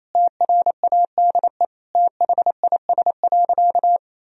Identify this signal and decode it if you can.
Morse code, heard as a keyed tone, about 28 wpm, 700 Hz. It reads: TRADE THIS.
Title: Sentences from Top 1000 Words - Review Part 4 - 28wpm